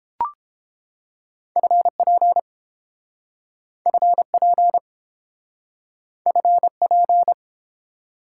Morse code, heard as a keyed tone, about 32 wpm, 700 Hz. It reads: FP FP FP